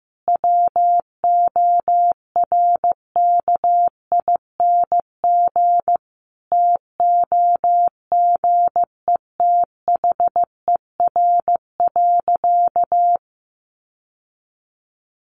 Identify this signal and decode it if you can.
Morse code, heard as a keyed tone, about 15 wpm, 700 Hz. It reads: WORKING TOGETHER.
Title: Sentences from Top 1200 Words 15wpm